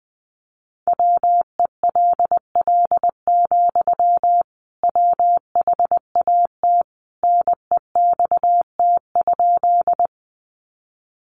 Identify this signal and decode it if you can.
Morse code, heard as a keyed tone, about 20 wpm, 700 Hz. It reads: WELL, WHAT NEXT?
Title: Sentences from Top 300 Words 20wpm